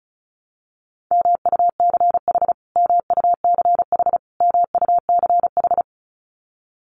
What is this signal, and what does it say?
Morse code, heard as a keyed tone, about 34 wpm, 700 Hz. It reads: MUCH MUCH MUCH
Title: Top 300 Words Repeated 35wpm